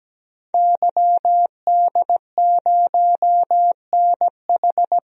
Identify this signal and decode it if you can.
Morse code, heard as a keyed tone, about 17 wpm, 700 Hz. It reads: YD0NH
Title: DXCC-100 Call Signs - 5 Characters - 17wpm